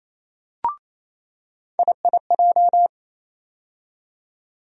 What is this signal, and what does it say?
Morse code, heard as a keyed tone, about 27 wpm, 700 Hz. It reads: IIJ
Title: USA Call Sign Postfixes - 3 Letters 28wpm